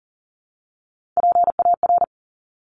Morse code, heard as a keyed tone, about 40 wpm, 700 Hz. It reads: PAR
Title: Top Three Letter Combinations in Words 40wpm